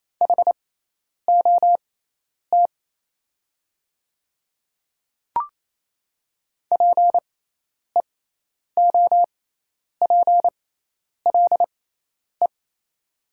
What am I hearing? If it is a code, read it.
Morse code, heard as a keyed tone, about 27 wpm, spaced as for 10 wpm, 690 Hz. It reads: HOT PEOPLE
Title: 6x Character Spacing - Top 100 Words - 28wpm